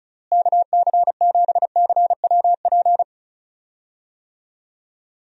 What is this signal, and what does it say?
Morse code, heard as a keyed tone, about 34 wpm, 700 Hz. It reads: KC7CWP
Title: USA Call Signs - 6 Characters - Repeated 3x - 35wpm